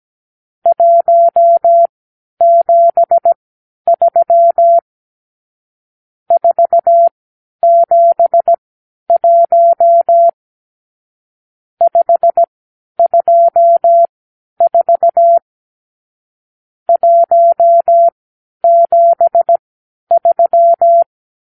Morse code, heard as a keyed tone, about 17 wpm, spaced as for 10 wpm, 680 Hz. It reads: 173 471 524 173
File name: k7qo / Lesson 52 (8)